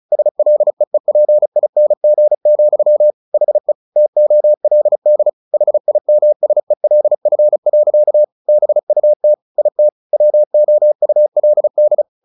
Morse code, heard as a keyed tone, about 34 wpm, 600 Hz. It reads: SLEEPING, HE TOLD HIMSELF. BUT IT WOULD
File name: the old man and the sea 2 35wpm